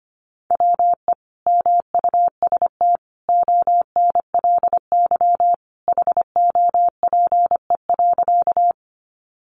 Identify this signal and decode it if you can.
Morse code, heard as a keyed tone, about 25 wpm, 700 Hz. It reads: WE MUST ONLY HOPE.
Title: Sentences from Top 1000 Words 25wpm